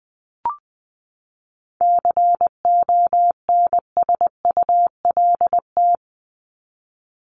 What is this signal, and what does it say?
Morse code, heard as a keyed tone, about 20 wpm, 700 Hz. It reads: CONSULT
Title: Top 1751-2000 Words 20wpm